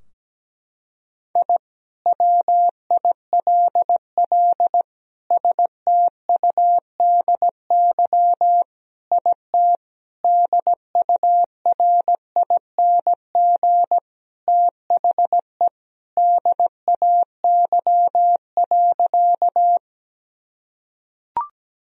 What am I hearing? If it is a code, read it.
Morse code, heard as a keyed tone, about 17 wpm, 700 Hz. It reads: I WILL STUDY IT DURING THE DAY.
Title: Sentences from Top 1000 Words - Review Part 2 - 17wpm